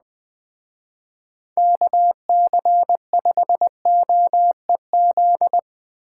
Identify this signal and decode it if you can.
Morse code, heard as a keyed tone, about 20 wpm, 700 Hz. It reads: KC5OEZ
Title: Speed Racing - USA 2x3 Call Signs 17wpm